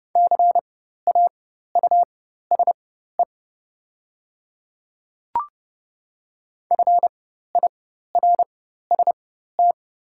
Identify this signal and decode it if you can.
Morse code, heard as a keyed tone, about 30 wpm, spaced as for 14 wpm, 700 Hz. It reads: CAUSE FIRST
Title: 4x Character Spacing - Top 200 Words - 30wpm